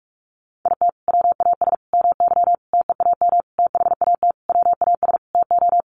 Morse code, heard as a keyed tone, about 44 wpm, 700 Hz. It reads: IT WAS MY TEAM THAT WAS TO
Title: Sentences from Top 1500 Words 45wpm